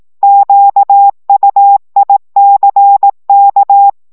Morse code, heard as a keyed tone, about 18 wpm, 800 Hz. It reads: QUICK
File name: szamolo / CW 15 baud